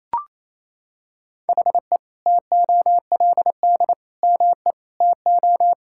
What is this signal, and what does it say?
Morse code, heard as a keyed tone, about 28 wpm, 700 Hz. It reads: HE TOLD ME TO
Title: Sentences from Top 1000 Words - Review Part 6 - 28wpm